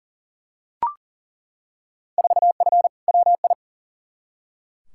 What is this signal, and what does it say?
Morse code, heard as a keyed tone, about 40 wpm, 700 Hz. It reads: 4F WI
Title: Field Day Exchanges - 40wpm